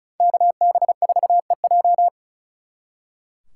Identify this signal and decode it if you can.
Morse code, heard as a keyed tone, about 45 wpm, 700 Hz. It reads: KB4EJ